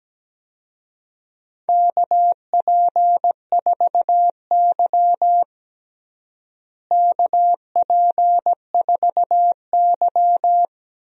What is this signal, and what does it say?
Morse code, heard as a keyed tone, about 17 wpm, 700 Hz. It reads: KP4Y KP4Y
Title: DXCC-100 Call Signs - 4 Characters - Repeated 2x - 17wpm